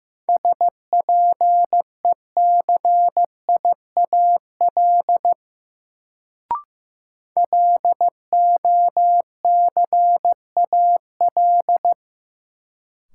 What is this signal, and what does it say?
Morse code, heard as a keyed tone, about 15 wpm, 700 Hz. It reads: SPECIAL LOCAL
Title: Sentences from Top 1000 Words - Review Part 3 - New Word Drill - 15wpm